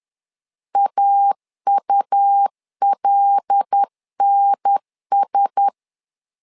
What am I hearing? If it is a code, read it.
Morse code, heard as a keyed tone, about 10 wpm, 800 Hz. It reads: AULNS
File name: LEZIONI / 5 LBVC / 5 L 10wpm 2259